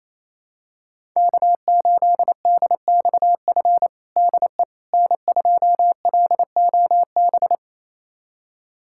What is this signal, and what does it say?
Morse code, heard as a keyed tone, about 28 wpm, 700 Hz. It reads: K8DXF DE N2LOB
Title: USA Callsign DE Callsign - 5 Characters - 28wpm